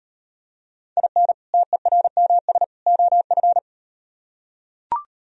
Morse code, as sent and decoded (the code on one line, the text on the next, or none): .. -. / - . .-. -- ... / --- ..-.
IN TERMS OF